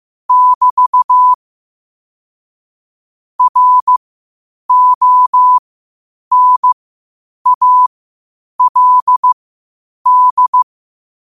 Morse code, heard as a keyed tone, about 15 wpm, spaced as for 8 wpm, 1000 Hz. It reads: <BT> RONALD
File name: qotdfiles / 8wpm